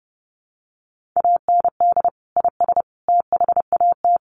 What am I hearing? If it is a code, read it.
Morse code, heard as a keyed tone, about 30 wpm, 700 Hz. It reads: AND IS THAT
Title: Sentences from Top 100 Words 30wpm